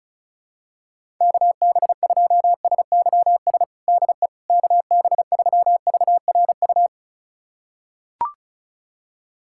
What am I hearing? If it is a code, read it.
Morse code, heard as a keyed tone, about 36 wpm, 700 Hz. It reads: KB2SYS DE KB3VRU
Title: USA Callsign DE Callsign - 6 Characters - 35wpm